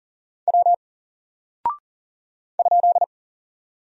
Morse code, heard as a keyed tone, about 40 wpm, 700 Hz. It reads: W ?
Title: Single Character (Letter:Number:Punctuation:Prosign) Alt 40wpm